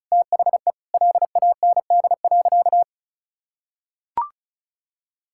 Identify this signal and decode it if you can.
Morse code, heard as a keyed tone, about 34 wpm, 700 Hz. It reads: THE LAND.
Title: Sentences from Top 1600 Words 35wpm